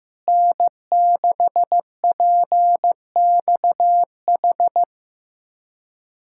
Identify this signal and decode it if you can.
Morse code, heard as a keyed tone, about 15 wpm, 700 Hz. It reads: N6PXH
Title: USA Callsign DE Callsign - 5 Characters - 15wpm